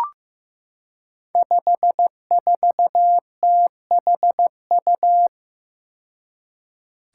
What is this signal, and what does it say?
Morse code, heard as a keyed tone, about 15 wpm, 700 Hz. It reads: 54THU
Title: Five Challenging Letter-Number Combinations 15wpm